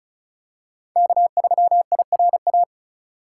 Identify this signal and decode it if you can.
Morse code, heard as a keyed tone, about 34 wpm, 700 Hz. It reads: K3IRA
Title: USA Call Signs - 5 Characters - 4x Character Spacing - 35wpm